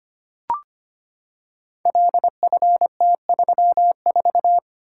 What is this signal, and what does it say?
Morse code, heard as a keyed tone, about 25 wpm, 700 Hz. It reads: LFT34
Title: Code Groups - 5 Letter:Numbers - 25wpm